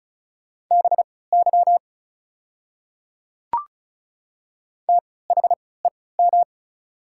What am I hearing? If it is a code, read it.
Morse code, heard as a keyed tone, about 34 wpm, spaced as for 20 wpm, 690 Hz. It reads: BY THEM
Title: 3x Character Spacing - Top 100 Words - 35wpm